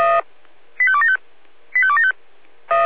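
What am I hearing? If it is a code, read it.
Five-tone selective call (ZVEI), heard as quick runs of five tones, tones of about 70 ms. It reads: 85275 85275
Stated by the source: ZVEI five-tone selective call 85276, received by SDR on the 496 channel.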